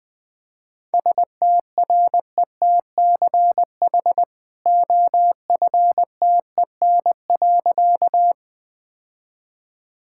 Morse code, heard as a keyed tone, about 21 wpm, 700 Hz. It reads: STRETCH OFTEN.
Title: Easy Sentences with Hints - 20wpm